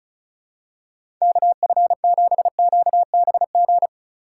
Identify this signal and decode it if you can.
Morse code, heard as a keyed tone, about 34 wpm, 700 Hz. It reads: KF7QBG
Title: USA Call Signs - 6 Characters - 3x Character Spacing - 35wpm